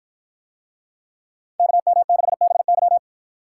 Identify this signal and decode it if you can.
Morse code, heard as a keyed tone, about 53 wpm, 700 Hz. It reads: KM6BY